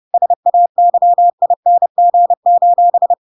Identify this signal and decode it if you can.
Morse code, heard as a keyed tone, about 30 wpm, 700 Hz. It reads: SAYING:
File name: Pinocchio / Pinocchio ch4 30 30 ik1hge